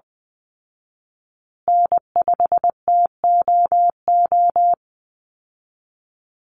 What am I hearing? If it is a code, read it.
Morse code, heard as a keyed tone, about 20 wpm, 700 Hz. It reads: N5TOO
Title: Call Signs Repeated 2x - 20wpm